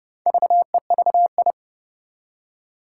Morse code, heard as a keyed tone, about 30 wpm, 700 Hz. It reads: VEVI
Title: Four Challenging Letter-Number Combinations 30wpm